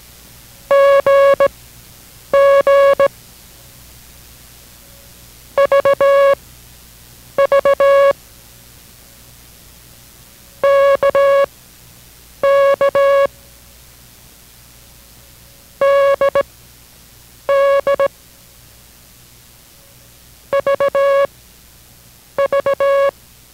Morse code, heard as a keyed tone, about 17 wpm, spaced as for 7 wpm, 560 Hz. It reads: GG VV KK DD VV